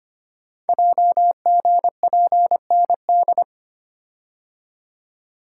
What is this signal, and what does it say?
Morse code, heard as a keyed tone, about 25 wpm, 700 Hz. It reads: JGPND
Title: Code Groups - 5 Letters - 25wpm